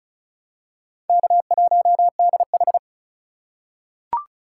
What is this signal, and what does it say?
Morse code, heard as a keyed tone, about 36 wpm, 700 Hz. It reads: K1DH
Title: USA Call Signs - 2x Character Spacing - 35wpm